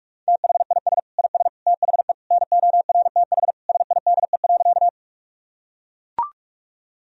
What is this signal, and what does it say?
Morse code, heard as a keyed tone, about 41 wpm, 700 Hz. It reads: THIS IS THE NORTH SIDE.
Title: Sentences from Top 1000 Words - Review Part 1 - 45wpm